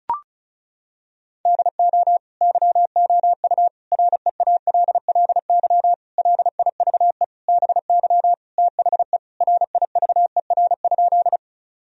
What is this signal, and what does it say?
Morse code, heard as a keyed tone, about 36 wpm, 700 Hz. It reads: DO YOU REALLY LIVE BY THE RIVER?
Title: Sentences from Top 1000 Words - Review Part 1 - 35wpm